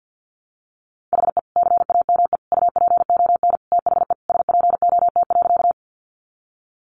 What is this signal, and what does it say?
Morse code, heard as a keyed tone, about 50 wpm, 700 Hz. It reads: HE CAME UPON THE SPOT.
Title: Sentences from Top 1100 Words 50wpm